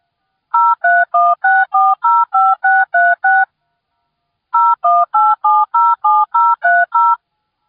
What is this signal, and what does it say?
DTMF, heard as telephone touch tones, tones of about 220 ms. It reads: 0316405636 018707030